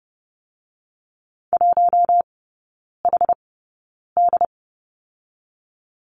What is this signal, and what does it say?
Morse code, heard as a keyed tone, about 30 wpm, spaced as for 9 wpm, 700 Hz. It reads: 1HD
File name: Lesson 19e - Letter w - Three Characters 30wpm